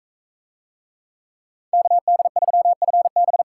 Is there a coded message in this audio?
Yes: Morse code from 1.7 to 3.4 s, heard as a keyed tone, about 42 wpm, 700 Hz. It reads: KD3FB